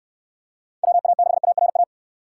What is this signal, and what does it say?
Morse code, heard as a keyed tone, about 69 wpm, 700 Hz. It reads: WA6ARA